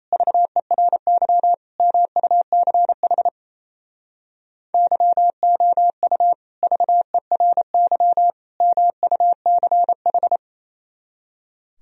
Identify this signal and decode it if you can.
Morse code, heard as a keyed tone, about 33 wpm, 700 Hz. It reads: VERY MUCH YOU VERY MUCH
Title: Speed-Racing - Sets of 3 Words 28wpm